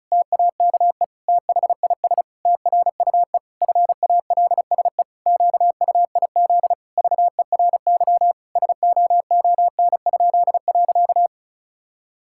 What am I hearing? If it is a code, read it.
Morse code, heard as a keyed tone, about 34 wpm, 700 Hz. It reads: TAKE THIS TRUE FALSE QUIZ VERY SOON?.
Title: Pangram Challenge v2 - 35wpm